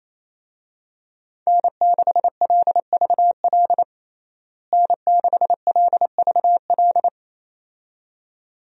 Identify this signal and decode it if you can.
Morse code, heard as a keyed tone, about 28 wpm, 700 Hz. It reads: N6LVL N6LVL
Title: USA Call Signs - 5 Characters - Repeated 2x - 28wpm